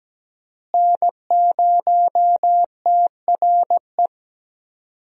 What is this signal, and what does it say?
Morse code, heard as a keyed tone, about 17 wpm, 700 Hz. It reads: N0TRE
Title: USA Call Signs - 5 Characters - Repeated 2x - 17wpm